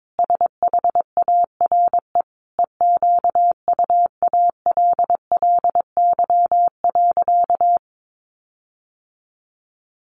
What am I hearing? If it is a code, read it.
Morse code, heard as a keyed tone, about 22 wpm, 700 Hz. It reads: SHARE EQUALLY.
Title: Sentences from Top 1400 Words 22wpm